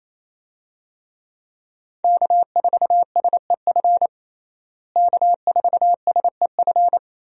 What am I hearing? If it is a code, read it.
Morse code, heard as a keyed tone, about 28 wpm, 700 Hz. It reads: K4SEF K4SEF
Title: USA Call Signs - 5 Characters - Repeated 2x - 28wpm